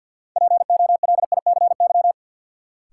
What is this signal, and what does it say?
Morse code, heard as a keyed tone, about 53 wpm, 700 Hz. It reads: POLICY
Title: Sentences from Top 1200 Words - New Word Drill Alt - 50wpm